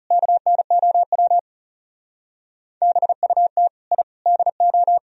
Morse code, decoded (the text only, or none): KNOW BUT I DO